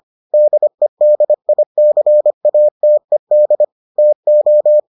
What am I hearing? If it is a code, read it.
Morse code, heard as a keyed tone, about 25 wpm, 600 Hz. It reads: DEDICATED TO